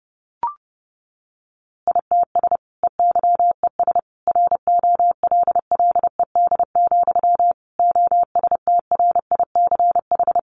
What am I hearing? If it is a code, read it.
Morse code, heard as a keyed tone, about 30 wpm, 700 Hz. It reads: ITS EYES ROLLED, OSTRICH
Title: A Sound Of Thunder 30wpm